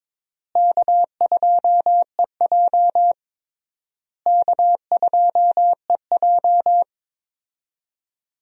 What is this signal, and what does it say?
Morse code, heard as a keyed tone, about 22 wpm, 700 Hz. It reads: K2EJ K2EJ
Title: USA Call Signs - 4 Characters - Repeated 3x - 22wpm